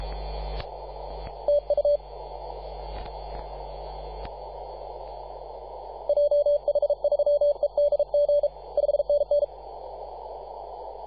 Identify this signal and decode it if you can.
Morse code, heard as a keyed tone, about 34 wpm, 580 Hz. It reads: TU JH3EDG 5NN
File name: posts / 2015-sept-24-2118z-t2gc-18-071-cw